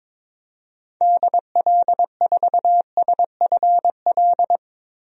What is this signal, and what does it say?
Morse code, heard as a keyed tone, about 22 wpm, 700 Hz. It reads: DL4SFL
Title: DXCC-100 Call Signs - 6 Characters - 22wpm